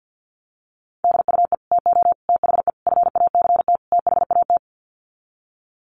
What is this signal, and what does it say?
Morse code, heard as a keyed tone, about 50 wpm, 700 Hz. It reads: DUE TO THE FACT THAT
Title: Sets of 5 Words Encore 50wpm